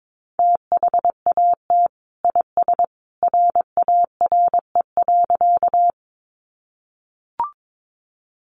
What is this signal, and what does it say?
Morse code, heard as a keyed tone, about 22 wpm, 700 Hz. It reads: THAT IS RARE.
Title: Sentences from Top 1100 Words 22wpm